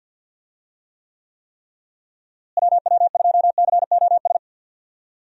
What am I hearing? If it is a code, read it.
Morse code, heard as a keyed tone, about 47 wpm, 700 Hz. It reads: WW2COS
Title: USA Call Signs - 6 Characters - Repeated 3x - 50wpm